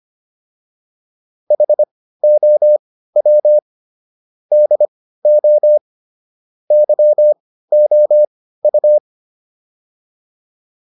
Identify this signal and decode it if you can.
Morse code, heard as a keyed tone, about 25 wpm, spaced as for 15 wpm, 600 Hz. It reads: HOW DO YOU
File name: WD204 15